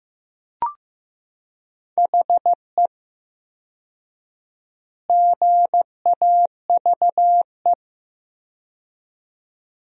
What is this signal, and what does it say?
Morse code, heard as a keyed tone, about 15 wpm, 700 Hz. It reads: HE GAVE
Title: 4x Word-Spacing - Sentences from Top 500 Words - 15wpm